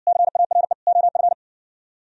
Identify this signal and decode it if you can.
Morse code, heard as a keyed tone, about 58 wpm, 700 Hz. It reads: CARE OF